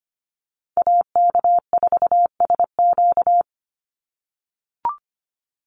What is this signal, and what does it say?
Morse code, heard as a keyed tone, about 25 wpm, 700 Hz. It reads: AK4SQ
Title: Call Signs - 25wpm